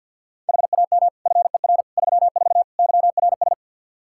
Morse code, heard as a keyed tone, about 51 wpm, 700 Hz. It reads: HAM FER 34 YRS
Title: Practice QSO - Exchange 3 - More Info - 50wpm